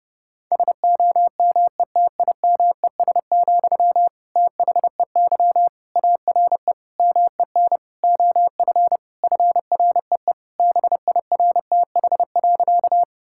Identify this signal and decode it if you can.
Morse code, heard as a keyed tone, about 30 wpm, 700 Hz. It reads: SOMETIMES, THEY ARE MEN OF FREE BIRTH.
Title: Sentences from Top 1000 Words - Review Part 5 - 30wpm